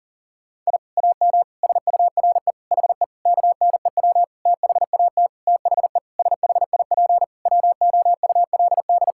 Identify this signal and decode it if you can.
Morse code, heard as a keyed tone, about 40 wpm, 700 Hz. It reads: I AM SURE HE KNEW THAT THE SHIP WOULD
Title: Sentences from Top 1000 Words - Review Part 6 - 40wpm